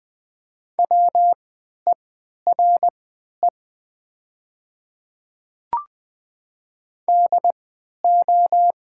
Morse code, heard as a keyed tone, about 20 wpm, spaced as for 11 wpm, 700 Hz. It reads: WERE DO
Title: 3x Character Spacing - Top 200 Words - 20wpm